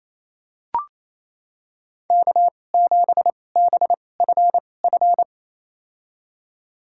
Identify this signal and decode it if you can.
Morse code, heard as a keyed tone, about 28 wpm, spaced as for 20 wpm, 700 Hz. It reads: K7BFF
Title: USA Call Signs - 5 Characters - 2x Character Spacing - 28wpm